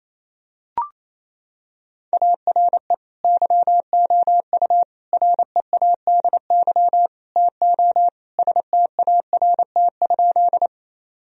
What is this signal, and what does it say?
Morse code, heard as a keyed tone, about 28 wpm, 700 Hz. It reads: ARE YOU READY TO START?
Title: Sentences from Top 1000 Words - Review Part 1 - 28wpm